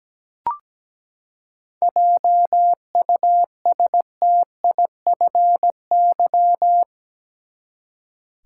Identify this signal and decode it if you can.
Morse code, heard as a keyed tone, about 17 wpm, 700 Hz. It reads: JUSTIFY
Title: Seven Letter Words - 17wpm